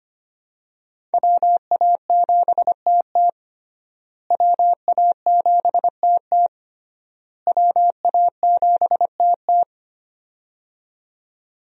Morse code, heard as a keyed tone, about 25 wpm, 700 Hz. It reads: WA7TT WA7TT WA7TT
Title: USA Call Signs - 5 Characters - Repeated 3x - 25wpm